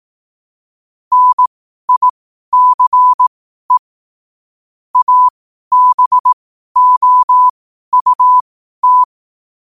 Morse code, heard as a keyed tone, about 18 wpm, spaced as for 12 wpm, 1000 Hz. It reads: NICE ABOUT
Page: Quotes for Fri, 15 Aug 2025 in Morse Code at 12 words per minute.